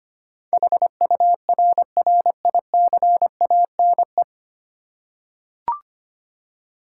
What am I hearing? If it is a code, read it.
Morse code, heard as a keyed tone, about 25 wpm, 700 Hz. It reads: HURRICANE
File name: 9 Letter Words - First 3 Letters Given - 25wpm